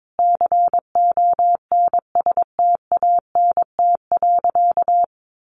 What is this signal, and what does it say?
Morse code, heard as a keyed tone, about 22 wpm, 700 Hz. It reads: CONSTANT.
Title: Sentences from Top 1500 Words 22wpm